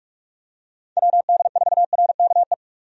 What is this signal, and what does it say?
Morse code, heard as a keyed tone, about 46 wpm, 700 Hz. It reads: WD4RKE
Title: USA Call Signs - 6 Characters - 45wpm